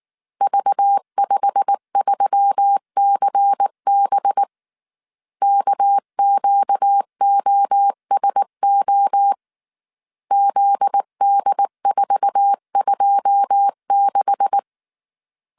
Morse code, heard as a keyed tone, about 19 wpm, 800 Hz. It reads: V53CB KQOSO ZD426
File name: CW / DET / 1 9 1500C S A 18wpm 2459